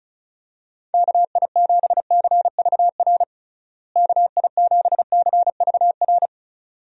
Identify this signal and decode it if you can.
Morse code, heard as a keyed tone, about 36 wpm, 700 Hz. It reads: KI7CVR KI7CVR
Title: USA Call Signs - 6 Characters - Repeated 2x - 35wpm